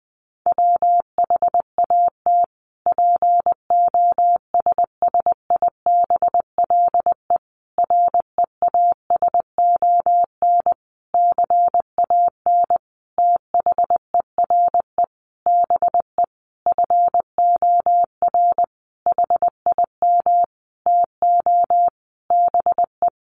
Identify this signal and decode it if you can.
Morse code, heard as a keyed tone, about 20 wpm, 700 Hz. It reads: WHAT POSSIBLE REASON CAN THERE BE FOR HIM TO BE
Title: Sentences from Top 1300 Words 20wpm